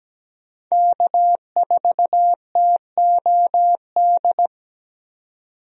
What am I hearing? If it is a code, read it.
Morse code, heard as a keyed tone, about 17 wpm, 700 Hz. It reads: K4TOD
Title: USA Call Signs - 5 Characters - 17wpm